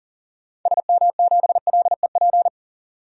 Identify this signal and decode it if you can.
Morse code, heard as a keyed tone, about 40 wpm, 700 Hz. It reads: SM7LEP
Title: DXCC-100 Call Signs - 6 Characters - 40wpm